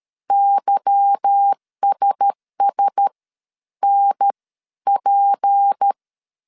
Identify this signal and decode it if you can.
Morse code, heard as a keyed tone, about 13 wpm, spaced as for 11 wpm, 800 Hz. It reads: YSS NP
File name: CW / DET / 1 9 1500C S C 12wpm 3811